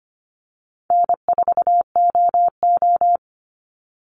Morse code, heard as a keyed tone, about 25 wpm, 700 Hz. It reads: N4OO
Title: Call Signs Repeated 2x - 25wpm